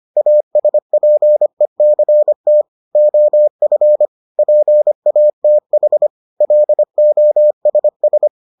Morse code, heard as a keyed tone, about 25 wpm, 600 Hz. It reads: ASPECT OF PATH LOSS